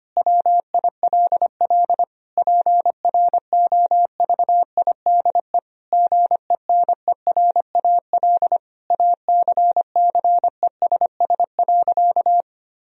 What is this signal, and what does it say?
Morse code, heard as a keyed tone, about 25 wpm, 700 Hz. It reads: WILL PROVIDE GENERAL ACCESS.
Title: Sentences from Top 1000 Words - Review Part 4 - 25wpm